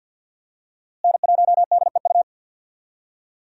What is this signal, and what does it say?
Morse code, heard as a keyed tone, about 52 wpm, 700 Hz. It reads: N1DEU
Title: USA Call Signs - 5 Characters - Repeated 2x - 50wpm